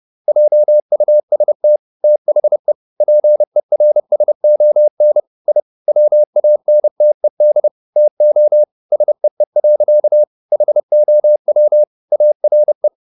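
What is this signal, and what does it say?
Morse code, heard as a keyed tone, about 30 wpm, 600 Hz. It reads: JUST THE PERSON I WANTED TO SEE. HOW ARE
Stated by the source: Morse code at 30 wpm, the novel Never by Ken Follett in English, from CW med Gnister.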